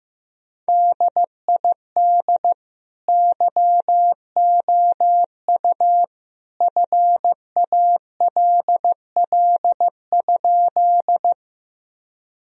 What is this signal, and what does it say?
Morse code, heard as a keyed tone, about 15 wpm, 700 Hz. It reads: DID YOU FALL?
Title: Sentences from Top 1000 Words - Review Part 2 - 15wpm